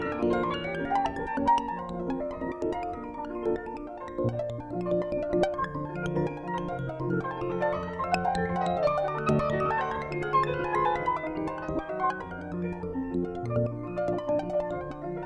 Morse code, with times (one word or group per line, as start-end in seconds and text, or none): none